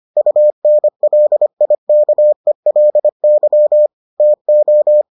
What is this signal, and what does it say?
Morse code, heard as a keyed tone, about 25 wpm, 600 Hz. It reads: UNLIKELY TO